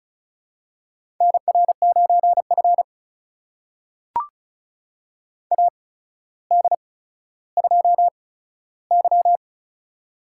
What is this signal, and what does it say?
Morse code, heard as a keyed tone, about 36 wpm, spaced as for 15 wpm, 700 Hz. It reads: NR9F AD2Y
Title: USA Call Signs - 4 Characters - 8x Character Spacing - 35wpm